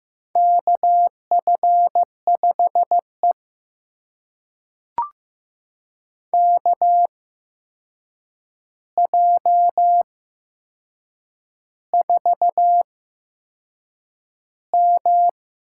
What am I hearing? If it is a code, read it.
Morse code, heard as a keyed tone, about 15 wpm, spaced as for 6 wpm, 700 Hz. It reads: KF5E KJ4M
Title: USA Call Signs - 4 Characters - 8x Character Spacing - 15wpm